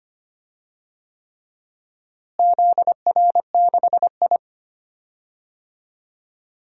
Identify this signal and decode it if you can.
Morse code, heard as a keyed tone, about 25 wpm, 700 Hz. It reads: ZR6I
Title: DXCC-100 Call Signs - 4 Characters - Repeated 3x - 25wpm